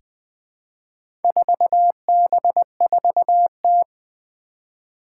Morse code, heard as a keyed tone, about 20 wpm, 700 Hz. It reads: 4B4T
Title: Four Challenging Letter-Number Combinations 20wpm